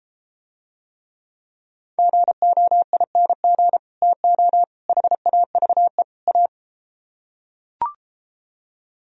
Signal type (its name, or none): Morse code